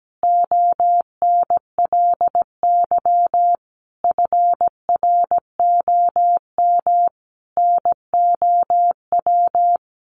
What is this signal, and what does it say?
Morse code, heard as a keyed tone, about 17 wpm, 700 Hz. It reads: ONLY FROM NOW
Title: Easy Phrases 17wpm